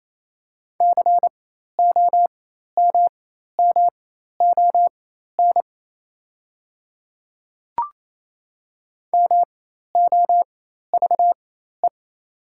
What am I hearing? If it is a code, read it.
Morse code, heard as a keyed tone, about 28 wpm, spaced as for 13 wpm, 690 Hz. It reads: COMMON MOVE
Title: 4x Character Spacing - Top 500 Words - 28wpm